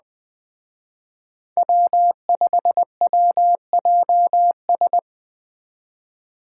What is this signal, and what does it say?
Morse code, heard as a keyed tone, about 20 wpm, 700 Hz. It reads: W5WJS